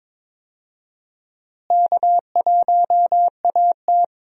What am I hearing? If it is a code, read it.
Morse code, heard as a keyed tone, about 21 wpm, 700 Hz. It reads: K1AT
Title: USA Call Signs - 4 Characters - Repeated 3x - 22wpm